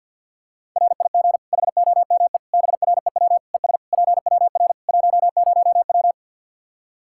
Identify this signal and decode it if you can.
Morse code, heard as a keyed tone, about 51 wpm, 700 Hz. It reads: RIG HOME BREW ES PWR 10W
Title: Practice QSO - Exchange 2 - Info - 50wpm